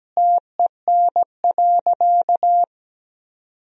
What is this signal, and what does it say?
Morse code, heard as a keyed tone, about 17 wpm, 700 Hz. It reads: TEN.
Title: Sentences from Top 1000 Words - Review Part 2 - 17wpm